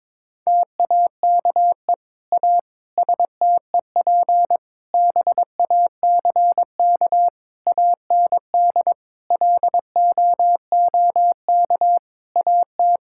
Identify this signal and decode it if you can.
Morse code, heard as a keyed tone, about 22 wpm, 700 Hz. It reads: TAKE A STEP BACK AND LOOK AT
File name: practice / 22wpm / Sentences with 8-Word Phrases - 22wpm